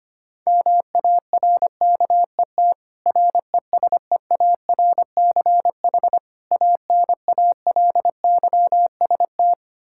Morse code, heard as a keyed tone, about 25 wpm, 700 Hz. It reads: MARKET RESEARCH ANALYST
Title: Practice QSO - Exchange 3 - More Info - 25wpm